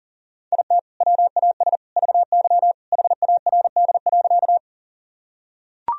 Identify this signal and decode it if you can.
Morse code, heard as a keyed tone, about 40 wpm, 700 Hz. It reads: IT WAS VY HARD.
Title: Sentences from Top 300 Words and Abbreviations 40wpm